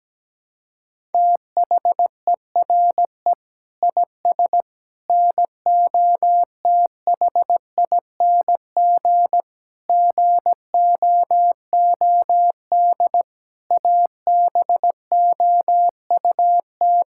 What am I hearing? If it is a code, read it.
Morse code, heard as a keyed tone, about 17 wpm, 700 Hz. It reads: THERE IS NOTHING GOOD ABOUT
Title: Sentences from Top 1000 Words - Review Part 5 - 17wpm